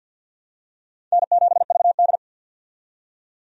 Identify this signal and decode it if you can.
Morse code, heard as a keyed tone, about 51 wpm, 700 Hz. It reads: N7VD
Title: USA Call Signs - 4 Characters - 50wpm